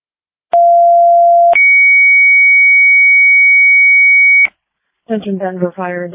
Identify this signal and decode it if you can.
Two-tone page, heard as a one-second tone, then a three-second tone, about 690 then 2250 Hz, with speech. It is heard from 0.5 to 4.5 s.
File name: Denver Fire